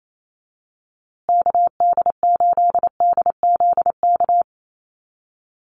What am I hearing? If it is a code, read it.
Morse code, heard as a keyed tone, about 28 wpm, 700 Hz. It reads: KD8DZK
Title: USA Call Signs Repeated 2x 28wpm